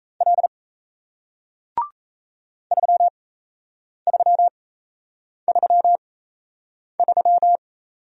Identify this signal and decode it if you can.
Morse code, heard as a keyed tone, about 37 wpm, spaced as for 8 wpm, 700 Hz. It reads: L 3333